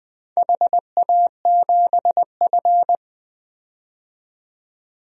Morse code, heard as a keyed tone, about 20 wpm, 700 Hz. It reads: HA7F
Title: DXCC-100 Call Signs - 4 Characters - 20wpm